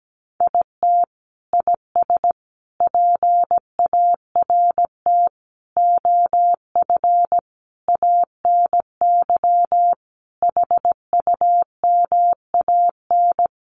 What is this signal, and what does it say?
Morse code, heard as a keyed tone, about 17 wpm, 700 Hz. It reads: IT IS PART OF ANY HUMAN
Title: Sentences from Top 1100 Words 17wpm